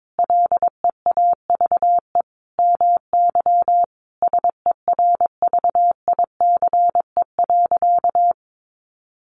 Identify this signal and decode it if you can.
Morse code, heard as a keyed tone, about 22 wpm, 700 Hz. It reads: LEAVE MY SERVICE.
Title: Sentences from Top 700 Words 22wpm